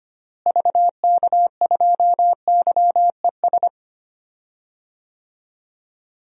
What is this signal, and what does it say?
Morse code, heard as a keyed tone, about 25 wpm, 700 Hz. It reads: VK2YES
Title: DXCC-100 Call Signs - 6 Characters - Repeated 2x - 25wpm